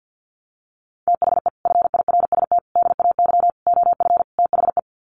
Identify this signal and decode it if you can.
Morse code, heard as a keyed tone, about 50 wpm, 700 Hz. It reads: THE FIRST DAY OF THE
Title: Sets of 5 Words Encore2 50wpm